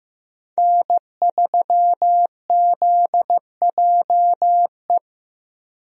Morse code, heard as a keyed tone, about 15 wpm, 700 Hz. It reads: N3ZJE
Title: USA Call Signs - 5 Characters - 15wpm